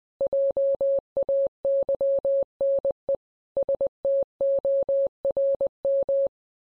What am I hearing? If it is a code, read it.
Morse code, heard as a keyed tone, about 20 wpm, 550 Hz. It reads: JAYNE STORM